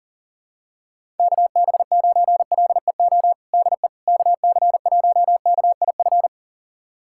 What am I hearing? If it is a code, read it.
Morse code, heard as a keyed tone, about 40 wpm, 700 Hz. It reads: KB9LEO DE KC1KIF
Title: USA Callsign DE Callsign - 6 Characters - 40wpm